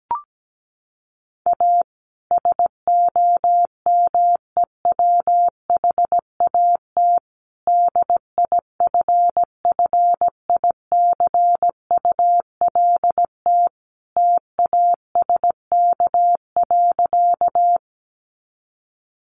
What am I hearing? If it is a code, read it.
Morse code, heard as a keyed tone, about 17 wpm, 700 Hz. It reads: A SOMEWHAT DIFFICULT TASK.
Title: Sentences from Top 1300 Words 17wpm